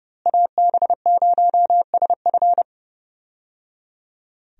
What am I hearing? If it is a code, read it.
Morse code, heard as a keyed tone, about 30 wpm, 700 Hz. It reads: AB0SF
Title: USA Call Signs - 5 Characters - Repeated 3x - 30wpm